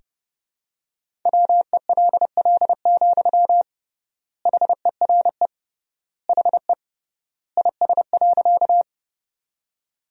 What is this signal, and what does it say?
Morse code, heard as a keyed tone, about 30 wpm, 700 Hz. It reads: WELL, HERE HE IS.
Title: 3x Word-Spacing - Sentences from Top 200 Words - 30wpm